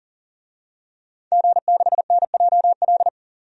Morse code, heard as a keyed tone, about 40 wpm, 700 Hz. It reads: G6NJL
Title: DXCC-100 Call Signs - 5 Characters - 40wpm